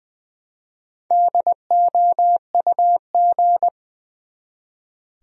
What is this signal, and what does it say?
Morse code, heard as a keyed tone, about 20 wpm, 700 Hz. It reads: DOUG
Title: USA Common Names - Max 5 Letters - 20wpm